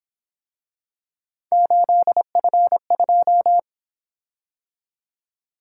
Morse code, heard as a keyed tone, about 27 wpm, 700 Hz. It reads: 8F2
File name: Speed-Racing - Three Letter-Number Combinations - 22wpm